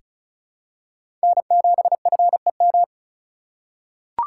Morse code, heard as a keyed tone, about 36 wpm, 700 Hz. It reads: N7FEM
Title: USA Call Signs - 3x Character Spacing - 35wpm